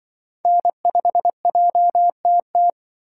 Morse code, heard as a keyed tone, about 23 wpm, 700 Hz. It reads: N5JTT
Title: Speed Racing - USA 1x3 Call Signs 20wpm